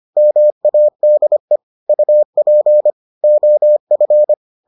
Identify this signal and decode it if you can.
Morse code, heard as a keyed tone, about 25 wpm, 600 Hz. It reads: MADE UP OF